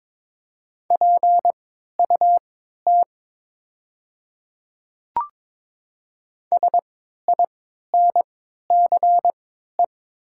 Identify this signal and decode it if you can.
Morse code, heard as a keyed tone, about 22 wpm, spaced as for 12 wpm, 700 Hz. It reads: PUT SINCE
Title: 3x Character Spacing - Top 300 Words - 22wpm